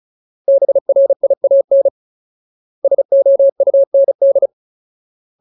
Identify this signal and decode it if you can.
Morse code, heard as a keyed tone, about 34 wpm, 550 Hz. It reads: BRIAN SOUND
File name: Ultimate Word List / 35WPM Ultimate